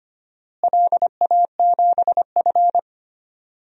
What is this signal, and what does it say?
Morse code, heard as a keyed tone, about 25 wpm, 700 Hz. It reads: LA7F
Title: DXCC-100 Call Signs - 4 Characters - Repeated 3x - 25wpm